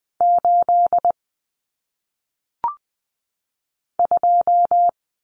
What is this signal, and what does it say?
Morse code, heard as a keyed tone, about 20 wpm, 700 Hz. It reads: 8 2
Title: Common Cut Numbers - 1-Digit 20wpm